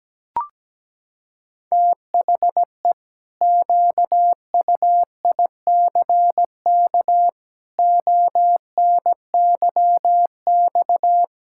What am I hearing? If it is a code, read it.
Morse code, heard as a keyed tone, about 17 wpm, 700 Hz. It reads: THE QUICK ONYX
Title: Pangram Challenge v2 - 17wpm